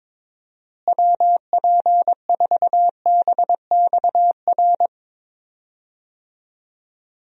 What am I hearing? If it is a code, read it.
Morse code, heard as a keyed tone, about 22 wpm, 700 Hz. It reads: WP4BXR
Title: USA Call Signs - 6 Characters - Repeated 2x - 22wpm